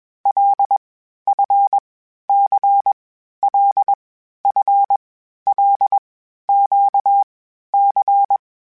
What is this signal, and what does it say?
Morse code, heard as a keyed tone, about 21 wpm, 800 Hz. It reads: L F C L F L Q C
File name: Basic Sets / Basic Letter Groups Collective7 / LFCYQ@20